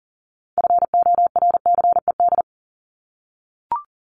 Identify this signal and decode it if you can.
Morse code, heard as a keyed tone, about 40 wpm, 700 Hz. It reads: FORCED
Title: ED Suffix - Top 100 Words with Base Words 40wpm